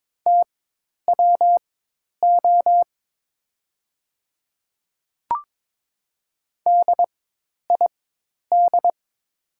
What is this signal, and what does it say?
Morse code, heard as a keyed tone, about 22 wpm, spaced as for 10 wpm, 700 Hz. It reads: TWO DID
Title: 4x Character Spacing - Top 300 Words - 22wpm